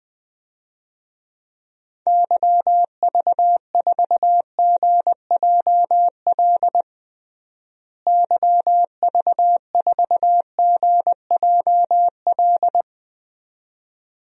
Morse code, heard as a keyed tone, about 21 wpm, 700 Hz. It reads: YV4GJL YV4GJL
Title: DXCC-100 Call Signs - 6 Characters - Repeated 2x - 20wpm